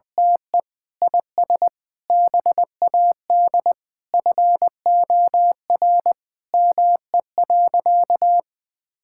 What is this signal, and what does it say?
Morse code, heard as a keyed tone, about 20 wpm, 700 Hz. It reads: TE IS BAD FOR ME.
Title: Sentences from Top 1600 Words 20wpm